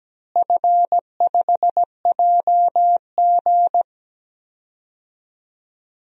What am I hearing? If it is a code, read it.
Morse code, heard as a keyed tone, about 17 wpm, 700 Hz. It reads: F5JG
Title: DXCC-100 Call Signs - 4 Characters - Repeated 2x - 17wpm